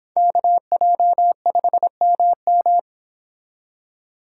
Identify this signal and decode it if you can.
Morse code, heard as a keyed tone, about 30 wpm, 700 Hz. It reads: KJ5MM